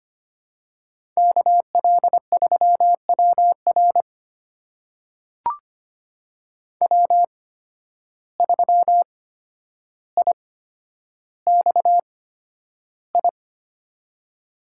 Morse code, heard as a keyed tone, about 25 wpm, spaced as for 11 wpm, 700 Hz. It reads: KL3WR W3IXI
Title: USA Call Signs - 5 Characters - 8x Character Spacing - 25wpm